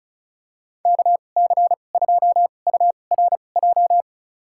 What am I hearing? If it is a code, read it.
Morse code, heard as a keyed tone, about 36 wpm, spaced as for 25 wpm, 700 Hz. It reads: KC2URJ
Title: USA Call Signs - 6 Characters - 2x Character Spacing - 35wpm